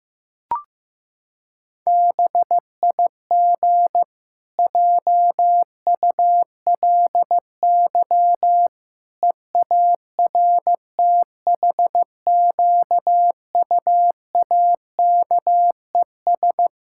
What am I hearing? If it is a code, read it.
Morse code, heard as a keyed tone, about 14 wpm, 700 Hz. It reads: BIG JULY EARTHQUAKES
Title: Pangram Challenge v2 - 15wpm